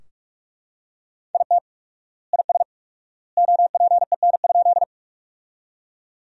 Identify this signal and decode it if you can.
Morse code, heard as a keyed tone, about 43 wpm, 700 Hz. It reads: IT IS OPEN?
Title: 4x Word-Spacing - Sentences from Top 300 Words - 45wpm